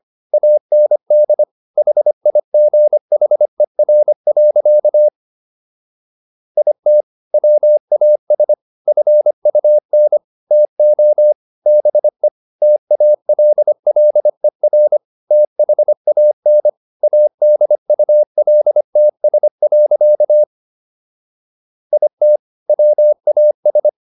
Morse code, heard as a keyed tone, about 25 wpm, 600 Hz. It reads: AND HIGHER. IT WAS FUN TO BE TALLER THAN ADULTS. IT WAS